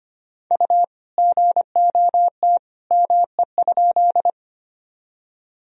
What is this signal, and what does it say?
Morse code, heard as a keyed tone, about 25 wpm, 700 Hz. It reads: U GOT ME?
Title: Sentences from Top 300 Words and Abbreviations 25wpm